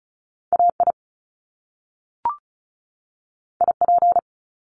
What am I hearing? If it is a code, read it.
Morse code, heard as a keyed tone, about 36 wpm, 700 Hz. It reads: AI IP
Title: Top Two Letter Combinations in Words 35wpm